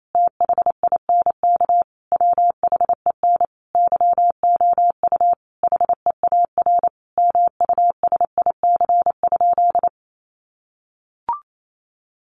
Morse code, heard as a keyed tone, about 28 wpm, 700 Hz. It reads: THINK WHEN YOU HEAR MUSIC?
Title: Sentences from Top 300 Words 28wpm